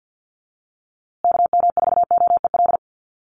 Morse code, heard as a keyed tone, about 50 wpm, 700 Hz. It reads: KM4OEL
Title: USA Call Signs Repeated 2x 50wpm